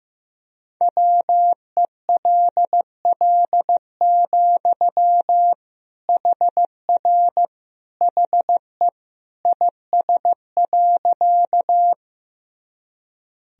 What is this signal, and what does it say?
Morse code, heard as a keyed tone, about 15 wpm, 700 Hz. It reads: WELL, HR HE IS.
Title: Sentences from Top 200 Words and Abbreviations 15wpm